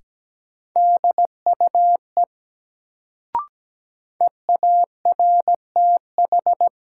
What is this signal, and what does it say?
Morse code, heard as a keyed tone, about 17 wpm, 700 Hz. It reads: DUE EARTH
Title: Sets of 5 Words Encore - New Word Drill - 17wpm